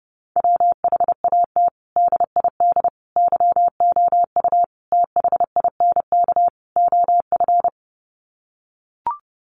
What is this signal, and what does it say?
Morse code, heard as a keyed tone, about 30 wpm, 700 Hz. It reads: WHAT DID YOU THINK OF
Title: Sets of 5 Words Encore2 30wpm